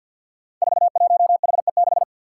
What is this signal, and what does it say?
Morse code, heard as a keyed tone, about 51 wpm, 700 Hz. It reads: 41HE6
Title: Five Challenging Letter-Number Combinations 50wpm